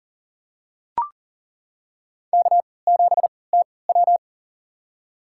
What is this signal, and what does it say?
Morse code, heard as a keyed tone, about 39 wpm, spaced as for 23 wpm, 700 Hz. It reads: K7TW
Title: USA Call Signs - 4 Characters - 3x Character Spacing - 40wpm